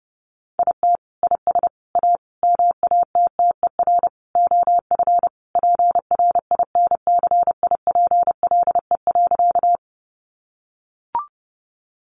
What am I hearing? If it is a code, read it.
Morse code, heard as a keyed tone, about 30 wpm, 700 Hz. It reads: IT IS A MATTER OF PRINCIPLE.
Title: Sentences from Top 1400 Words 30wpm